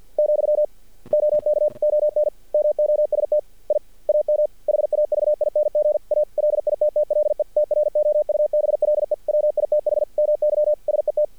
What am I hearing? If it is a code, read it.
Morse code, heard as a keyed tone, about 50 wpm, 600 Hz. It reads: , OK ON MOST I AM HAVING A LITTLE TROUBLE WITH MY SET
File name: archived / CRSnet